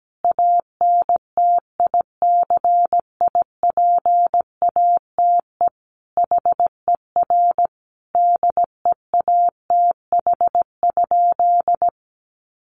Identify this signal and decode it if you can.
Morse code, heard as a keyed tone, about 17 wpm, 700 Hz. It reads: ANTICIPATE HER DEATH?
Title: Sentences from Top 1500 Words 17wpm